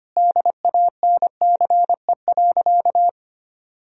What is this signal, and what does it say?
Morse code, heard as a keyed tone, about 25 wpm, 700 Hz. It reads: DANCE.
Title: Sentences from Top 1000 Words - Review Part 4 - 25wpm